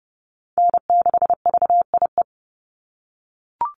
Morse code, heard as a keyed tone, about 30 wpm, 700 Hz. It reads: N6VIE
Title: USA Call Signs 30wpm